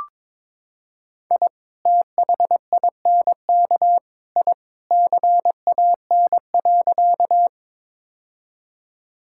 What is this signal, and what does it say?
Morse code, heard as a keyed tone, about 22 wpm, 700 Hz. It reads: I THINK I CAN.
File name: practice / 22wpm / Sentences with 3-Word Phrases - 22wpm